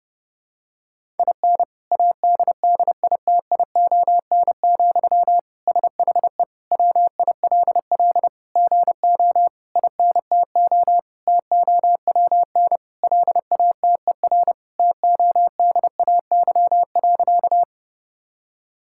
Morse code, heard as a keyed tone, about 32 wpm, 700 Hz. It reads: IN ADDITION, SHE WILL GO INTO TOWN LATER TODAY.
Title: Sentences from Top 1000 Words - Review Part 6 - 30wpm